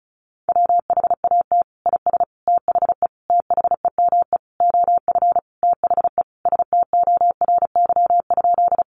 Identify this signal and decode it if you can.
Morse code, heard as a keyed tone, about 34 wpm, 700 Hz. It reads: WHAT IS THE THEME OF THE STORY?
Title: Sentences from Top 1100 Words 35wpm